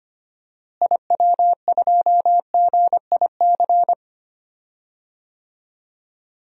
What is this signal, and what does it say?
Morse code, heard as a keyed tone, about 25 wpm, 700 Hz. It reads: IW2GIC
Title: DXCC-100 Call Signs - 6 Characters - Repeated 2x - 25wpm